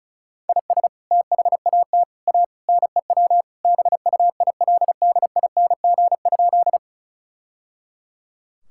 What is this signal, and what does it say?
Morse code, heard as a keyed tone, about 34 wpm, 700 Hz. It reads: IS THAT A NEW BUILDING?
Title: Sentences from Top 1000 Words - Review Part 5 - 35wpm